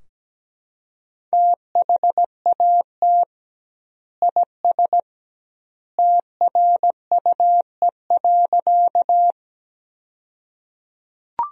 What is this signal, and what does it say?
Morse code, heard as a keyed tone, about 17 wpm, 700 Hz. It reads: THAT IS TRUE.
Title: 2x Word-Spacing - Sentences from Top 400 Words - 17wpm